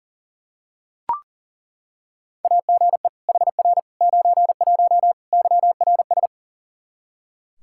Morse code, heard as a keyed tone, about 40 wpm, 700 Hz. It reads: AGE HR 91 YRS
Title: Practice QSO - Exchange 3 - More Info - 40wpm